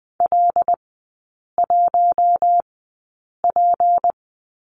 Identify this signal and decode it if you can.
Morse code, heard as a keyed tone, about 20 wpm, spaced as for 8 wpm, 700 Hz. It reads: L1P
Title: 152 Lesson 25e - Letter p - Three Characters 20wpm